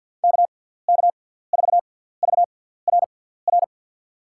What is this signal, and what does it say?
Morse code, heard as a keyed tone, about 53 wpm, spaced as for 17 wpm, 700 Hz. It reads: KK4VRR
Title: USA Call Signs - 6 Characters - 6x Character Spacing - 50wpm